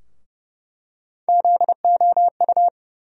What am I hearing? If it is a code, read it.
Morse code, heard as a keyed tone, about 30 wpm, 700 Hz. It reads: ZOU
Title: USA Call Sign Postfixes - 3 Letters 30wpm